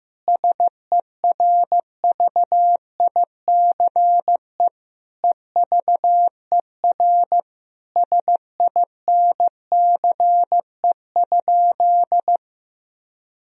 Morse code, heard as a keyed tone, about 15 wpm, 700 Hz. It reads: SERVICE EVER SINCE?
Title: Sentences from Top 1000 Words - Review Part 4 - 15wpm